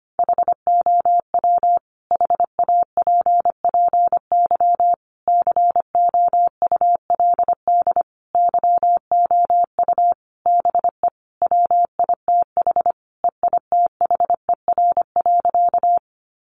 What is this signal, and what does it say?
Morse code, heard as a keyed tone, about 25 wpm, 700 Hz. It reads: HOW HAPPY COULD YOU BE WITH EITHER.
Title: Sentences from Top 1000 Words 25wpm